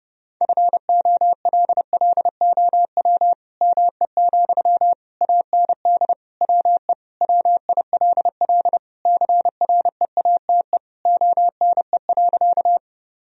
Morse code, heard as a keyed tone, about 30 wpm, 700 Hz. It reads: FOLLOW ME, AND WE WILL CREATE ONE.
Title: Sentences from Top 1000 Words - Review Part 3 - 30wpm